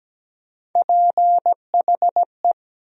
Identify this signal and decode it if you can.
Morse code, heard as a keyed tone, about 17 wpm, 700 Hz. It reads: PHE
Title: USA Call Sign Postfixes - 3 Letters 17wpm